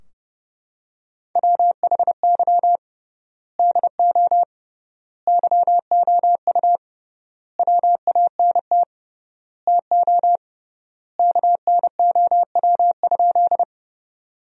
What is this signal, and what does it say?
Morse code, heard as a keyed tone, about 30 wpm, 700 Hz. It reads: WHY DO YOU WANT TO KNOW?
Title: 3x Word-Spacing - Sentences from Top 200 Words - 30wpm